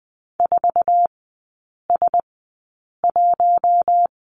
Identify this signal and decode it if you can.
Morse code, heard as a keyed tone, about 20 wpm, spaced as for 8 wpm, 700 Hz. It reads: 4S1
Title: 081 Lesson 14e - Number 2 - Three Characters 20wpm